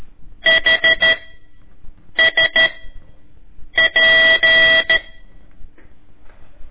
Morse code, heard as a keyed tone, about 13 wpm, 1950 Hz. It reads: H S P
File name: ik1weg2 / CORSI CW / lez4-1